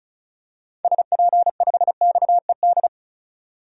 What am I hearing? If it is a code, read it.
Morse code, heard as a keyed tone, about 34 wpm, 700 Hz. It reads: SP5XED
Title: DXCC-100 Call Signs - 6 Characters - 35wpm